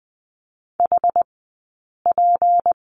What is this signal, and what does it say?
Morse code, heard as a keyed tone, about 20 wpm, spaced as for 8 wpm, 700 Hz. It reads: HP
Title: 211 Lesson 34d - Number 8 - Two Characters 20wpm